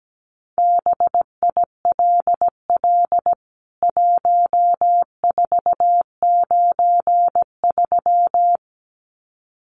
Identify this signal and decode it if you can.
Morse code, heard as a keyed tone, about 17 wpm, 700 Hz. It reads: BILL 1493
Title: CWT Exchange 17wpm